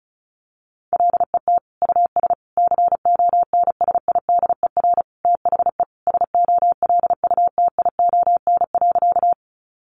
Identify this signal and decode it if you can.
Morse code, heard as a keyed tone, about 34 wpm, 700 Hz. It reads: LET US CONSIDER THE SOLUTION.
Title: Sentences from Top 1300 Words 35wpm